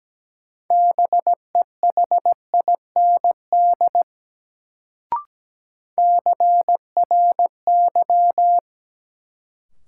Morse code, heard as a keyed tone, about 17 wpm, 700 Hz. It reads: BEHIND CRY
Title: Sentences from Top 500 Words - New Word Drill - 17wpm